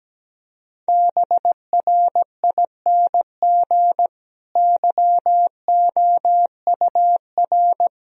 Morse code, heard as a keyed tone, about 17 wpm, 700 Hz. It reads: BRING YOUR
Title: Sentences from Top 1000 Words - Review Part 3 - 17wpm